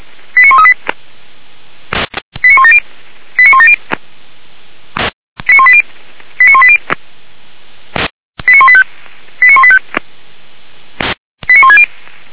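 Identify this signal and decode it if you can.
Five-tone selective call (ZVEI), heard as quick runs of five tones, tones of about 70 ms. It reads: 89138 89170 89170 89180 89180 89185 89185 89160
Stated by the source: Five-tone ZVEI selective call received by SDR, from the audio archive of the Feuerwehr Orlamünde fire brigade.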